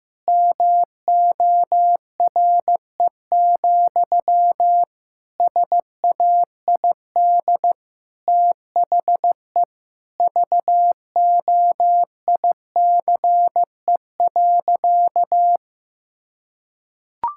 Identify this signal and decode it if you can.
Morse code, heard as a keyed tone, about 16 wpm, 700 Hz. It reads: MORE, SAID THE VOICE.
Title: Sentences from Top 1600 Words 15wpm